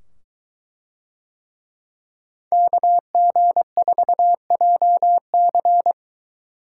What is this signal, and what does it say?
Morse code, heard as a keyed tone, about 21 wpm, 700 Hz. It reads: KG4JC